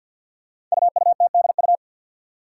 Morse code, heard as a keyed tone, about 50 wpm, 700 Hz. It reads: UUTDU
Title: Code Groups - 5 Letters - 50wpm